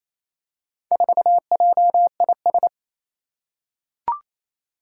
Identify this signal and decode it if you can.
Morse code, heard as a keyed tone, about 28 wpm, 700 Hz. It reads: 4JIS
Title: Four Challenging Letter-Number Combinations 28wpm